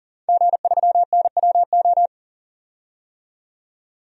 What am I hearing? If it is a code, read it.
Morse code, heard as a keyed tone, about 40 wpm, 700 Hz. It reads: G3NWO
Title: DXCC-100 Call Signs - 5 Characters - 40wpm